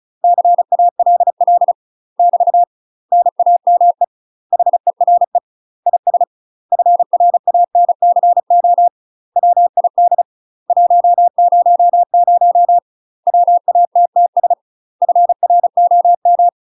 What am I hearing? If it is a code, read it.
Morse code, heard as a keyed tone, about 35 wpm, 700 Hz. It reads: CALL <BT> NAME HERE IS FRANCO WID 100 WATTS FROM
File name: UK MASTER QSOs TRAINING / 35WPM / UK MASTER QSOs 0020